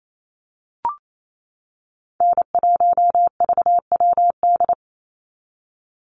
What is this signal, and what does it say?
Morse code, heard as a keyed tone, about 28 wpm, 700 Hz. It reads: N1VWD
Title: NA SOTA Chaser Callsigns 28wpm